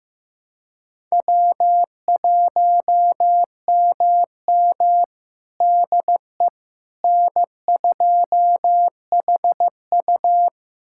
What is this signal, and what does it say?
Morse code, heard as a keyed tone, about 15 wpm, 700 Hz. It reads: W1MM DE N2HU
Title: USA Callsign DE Callsign - 4 Characters - 15wpm